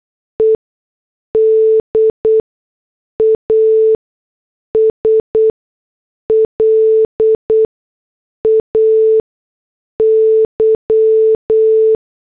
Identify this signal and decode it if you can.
Morse code, heard as a keyed tone, about 8 wpm, spaced as for 6 wpm, 440 Hz. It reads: ED AS LAY